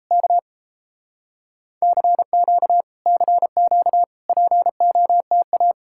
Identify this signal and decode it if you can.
Morse code, heard as a keyed tone, about 38 wpm, 700 Hz. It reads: K CQ CQ POTA